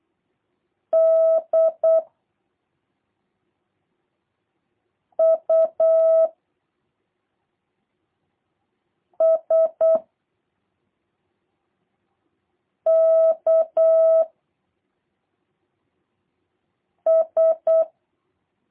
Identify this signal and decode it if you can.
Morse code, heard as a keyed tone, about 8 wpm, 650 Hz. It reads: D U S K S
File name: jh8cbh / 2023cw / review1-3